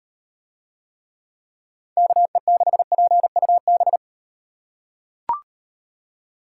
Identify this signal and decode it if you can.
Morse code, heard as a keyed tone, about 38 wpm, 700 Hz. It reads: KE6PUB